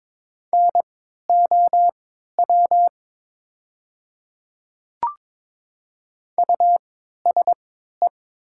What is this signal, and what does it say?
Morse code, heard as a keyed tone, about 22 wpm, spaced as for 12 wpm, 700 Hz. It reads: NOW USE
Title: 3x Character Spacing - Top 200 Words - 22wpm